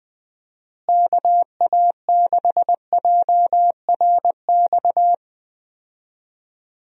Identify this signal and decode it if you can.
Morse code, heard as a keyed tone, about 20 wpm, 700 Hz. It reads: KA6JRX
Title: USA Call Signs - 6 Characters - Repeated 3x - 20wpm